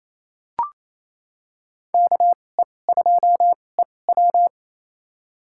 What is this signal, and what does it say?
Morse code, heard as a keyed tone, about 28 wpm, spaced as for 20 wpm, 700 Hz. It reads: KE2EW
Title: USA Call Signs - 5 Characters - 2x Character Spacing - 28wpm